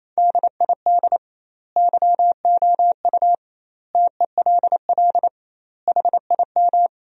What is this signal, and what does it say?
Morse code, heard as a keyed tone, about 28 wpm, 700 Hz. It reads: DID YOU TELL HIM
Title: 2x Word-Spacing - Sentences from Top 400 Words - 28wpm